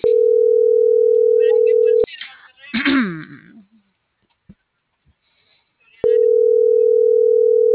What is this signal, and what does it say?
Ringback tone (North America), 440 + 480 Hz.